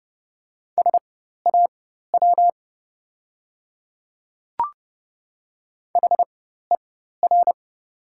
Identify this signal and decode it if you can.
Morse code, heard as a keyed tone, about 30 wpm, spaced as for 14 wpm, 700 Hz. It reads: SAW HER
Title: 4x Character Spacing - Top 500 Words - 30wpm